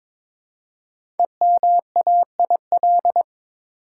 Morse code, heard as a keyed tone, about 22 wpm, 700 Hz. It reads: EMAIL
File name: 5 Letter Words - First 3 Letters Given - 22wpm